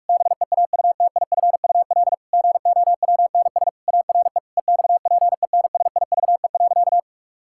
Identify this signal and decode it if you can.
Morse code, heard as a keyed tone, about 46 wpm, 700 Hz. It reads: BEAUTIFUL GOWNS ARE EXPENSIVE.
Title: Pangram Challenge v2 - 45wpm